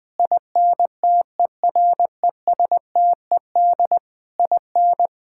Morse code, heard as a keyed tone, about 20 wpm, 700 Hz. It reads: INTERESTED IN
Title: Sentences from Top 1000 Words - Review Part 6 - 20wpm